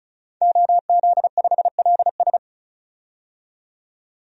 Morse code, heard as a keyed tone, about 35 wpm, 700 Hz. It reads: OZ5LS
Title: DXCC-100 Call Signs - 5 Characters - 35wpm